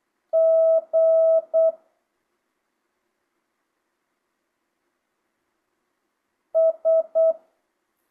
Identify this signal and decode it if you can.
Morse code, heard as a keyed tone, about 8 wpm, 650 Hz. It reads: G S
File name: jh8cbh / 2025cw